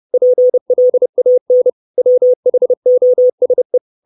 Morse code, heard as a keyed tone, about 30 wpm, 500 Hz. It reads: PLAN WHOSE
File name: 632 words / 632 Words 30WPM